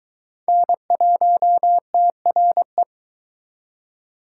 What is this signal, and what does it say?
Morse code, heard as a keyed tone, about 22 wpm, 700 Hz. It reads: N1TRE